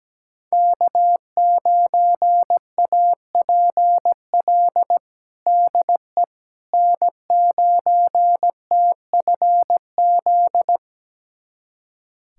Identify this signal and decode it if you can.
Morse code, heard as a keyed tone, about 17 wpm, 700 Hz. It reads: K9APL DE N9TFZ
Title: USA Callsign DE Callsign - 5 Characters - 17wpm